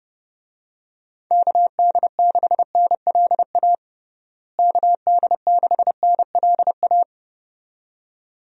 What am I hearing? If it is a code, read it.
Morse code, heard as a keyed tone, about 29 wpm, 700 Hz. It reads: KD6NLA KD6NLA
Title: USA Call Signs - 6 Characters - Repeated 2x - 30wpm